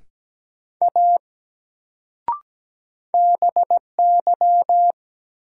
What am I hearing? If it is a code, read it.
Morse code, heard as a keyed tone, about 17 wpm, 700 Hz. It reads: A BY
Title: Sentences from Top 100 Words - New Word Drill - 17wpm